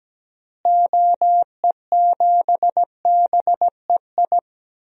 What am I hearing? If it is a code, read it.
Morse code, heard as a keyed tone, about 17 wpm, 700 Hz. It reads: OE7BEI
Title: DXCC-100 Call Signs - 6 Characters - Repeated 2x - 17wpm